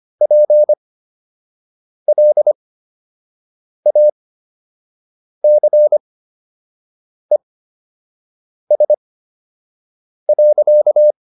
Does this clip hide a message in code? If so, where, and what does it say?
Yes: Morse code from 0.2 to 11.1 s, heard as a keyed tone, about 25 wpm, spaced as for 6 wpm, 600 Hz. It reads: PLACES.